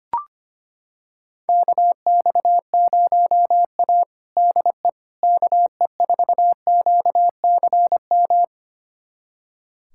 Morse code, heard as a keyed tone, about 25 wpm, 700 Hz. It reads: KX0A DE KE4QCM
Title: USA Callsign DE Callsign 25wpm